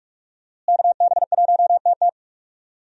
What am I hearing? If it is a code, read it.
Morse code, heard as a keyed tone, about 46 wpm, 700 Hz. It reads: KB1TT
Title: USA Call Signs - 5 Characters - Repeated 2x - 45wpm